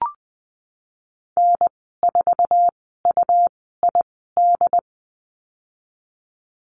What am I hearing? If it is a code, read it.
Morse code, heard as a keyed tone, about 20 wpm, spaced as for 15 wpm, 700 Hz. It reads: N4UID
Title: 122 Lesson 20g - Number 3 - Call Signs 20wpm